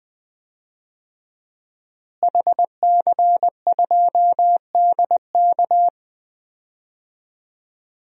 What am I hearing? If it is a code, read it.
Morse code, heard as a keyed tone, about 20 wpm, 700 Hz. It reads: HC2DK
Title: DXCC-100 Call Signs - 5 Characters - Repeated 3x - 20wpm